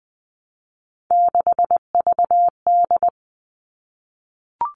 Morse code, heard as a keyed tone, about 20 wpm, 700 Hz. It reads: 6VD
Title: All Cut Numbers - 3-Digit 20wpm